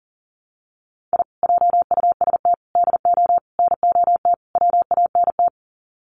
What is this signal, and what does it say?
Morse code, heard as a keyed tone, about 40 wpm, 700 Hz. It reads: I JUST DO NOT WANT
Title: Sets of 5 Words Encore2 40wpm